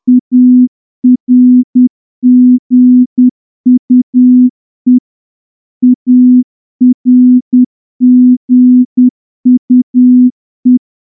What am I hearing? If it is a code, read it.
Morse code, heard as a keyed tone, about 10 wpm, 250 Hz. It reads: ARGUE ARGUE
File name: morse-training-10WPM